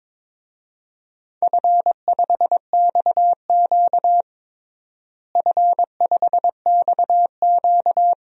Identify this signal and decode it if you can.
Morse code, heard as a keyed tone, about 22 wpm, 700 Hz. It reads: F5XQ F5XQ
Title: DXCC-100 Call Signs - 4 Characters - Repeated 3x - 22wpm